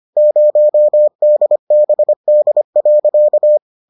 Morse code, heard as a keyed tone, about 25 wpm, 600 Hz. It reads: 0 DBD.